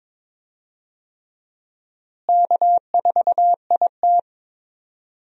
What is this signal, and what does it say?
Morse code, heard as a keyed tone, about 22 wpm, 700 Hz. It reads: K4IT
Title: USA Call Signs - 4 Characters - Repeated 3x - 22wpm